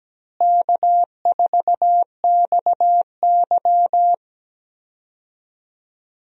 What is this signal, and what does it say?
Morse code, heard as a keyed tone, about 17 wpm, 700 Hz. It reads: K4XY
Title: USA Call Signs - 4 Characters - Repeated 3x - 17wpm